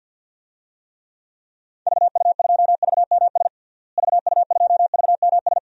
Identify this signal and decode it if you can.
Morse code, heard as a keyed tone, about 48 wpm, 700 Hz. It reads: VU2VMS VU2VMS
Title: DXCC-100 Call Signs - 6 Characters - Repeated 2x - 50wpm